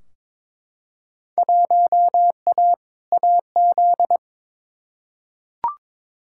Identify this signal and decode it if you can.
Morse code, heard as a keyed tone, about 22 wpm, 700 Hz. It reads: 1A AZ
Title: Field Day Exchanges - 22wpm